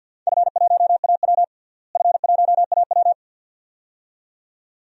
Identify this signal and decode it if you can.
Morse code, heard as a keyed tone, about 51 wpm, 700 Hz. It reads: F1AW F1AW
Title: DXCC-100 Call Signs - 4 Characters - Repeated 2x - 50wpm